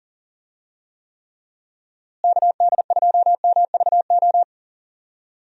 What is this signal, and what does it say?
Morse code, heard as a keyed tone, about 40 wpm, 700 Hz. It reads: KD2MVO
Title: USA Call Signs - 6 Characters - Repeated 2x - 40wpm